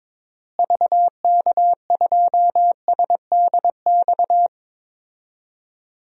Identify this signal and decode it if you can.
Morse code, heard as a keyed tone, about 22 wpm, 700 Hz. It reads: VK2SDX
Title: DXCC-100 Call Signs - 6 Characters - Repeated 3x - 22wpm